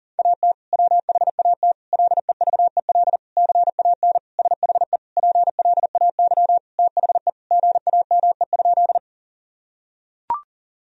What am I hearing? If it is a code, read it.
Morse code, heard as a keyed tone, about 40 wpm, 690 Hz. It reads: AT WHAT LEVEL CAN SHE PLAY THE GAME?
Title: Sentences from Top 1000 Words - Review Part 4 - 40wpm